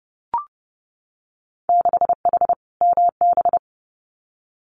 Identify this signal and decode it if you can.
Morse code, heard as a keyed tone, about 30 wpm, 700 Hz. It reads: 6H MB
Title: Winter Field Day Exchanges Encore 30wpm